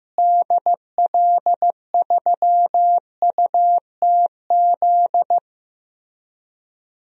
Morse code, heard as a keyed tone, about 15 wpm, 700 Hz. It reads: DL3UTZ
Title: DXCC-100 Call Signs - 6 Characters - Repeated 3x - 15wpm